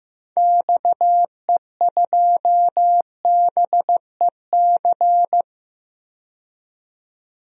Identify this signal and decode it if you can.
Morse code, heard as a keyed tone, about 16 wpm, 700 Hz. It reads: XE2BEC
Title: DXCC-100 Call Signs - 6 Characters - 15wpm